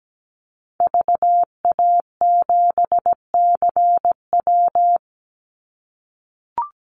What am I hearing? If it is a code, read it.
Morse code, heard as a keyed tone, about 17 wpm, 700 Hz. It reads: VA7CW
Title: Call Signs - 17wpm